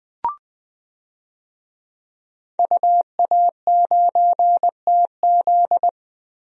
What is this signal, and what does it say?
Morse code, heard as a keyed tone, about 20 wpm, 700 Hz. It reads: UA9TZ
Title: DXCC-100 Call Signs - 5 Characters - 20wpm